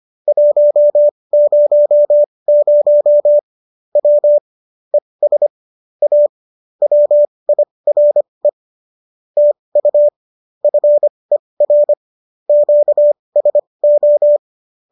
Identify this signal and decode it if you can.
Morse code, heard as a keyed tone, about 25 wpm, spaced as for 20 wpm, 590 Hz. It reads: 100 W ES A WIRE TU FER QSO